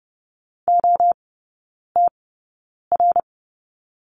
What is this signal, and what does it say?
Morse code, heard as a keyed tone, about 30 wpm, spaced as for 9 wpm, 700 Hz. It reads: OTR
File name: Lesson 16e - Letter c - Three Characters 30wpm